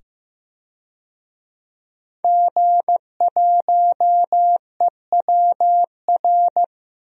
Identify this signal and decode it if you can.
Morse code, heard as a keyed tone, about 15 wpm, 700 Hz. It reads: G1EWR
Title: DXCC-100 Call Signs - 5 Characters - 15wpm